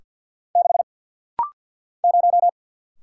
Morse code, heard as a keyed tone, about 50 wpm, 700 Hz. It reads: 6 0
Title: Single Letter-Number - Rapid-Fire 50wpm